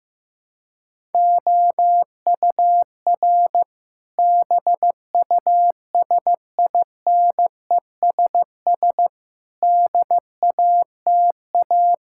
Morse code, heard as a keyed tone, about 15 wpm, 700 Hz. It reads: OUR BUSINESS DATA
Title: Sentences from Top 1000 Words - Review Part 6 - 15wpm